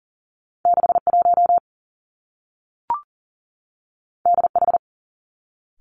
Morse code, heard as a keyed tone, about 40 wpm, 700 Hz. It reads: 6 1 D H
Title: Two Challenging Letter-Number Combinations 40wpm